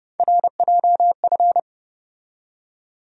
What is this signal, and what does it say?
Morse code, heard as a keyed tone, about 31 wpm, 700 Hz. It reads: RJF
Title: USA Call Sign Postfixes - 3 Letters 30wpm